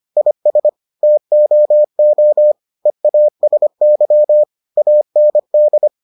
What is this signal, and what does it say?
Morse code, heard as a keyed tone, about 25 wpm, 600 Hz. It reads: IS TOO EASY AND